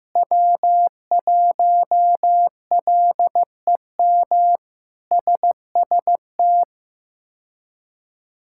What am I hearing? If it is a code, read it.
Morse code, heard as a keyed tone, about 15 wpm, 700 Hz. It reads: W1LEM SST
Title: K1USN SST QSO Practice - 15wpm